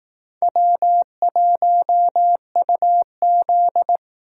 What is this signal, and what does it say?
Morse code, heard as a keyed tone, about 18 wpm, 700 Hz. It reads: W1UZ
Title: Speed Racing - USA Call Signs 15wpm